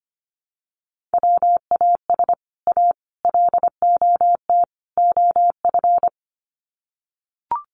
Morse code, heard as a keyed tone, about 25 wpm, 700 Hz. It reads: WAS A LOT OF
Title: Sets of 4 Words 25wpm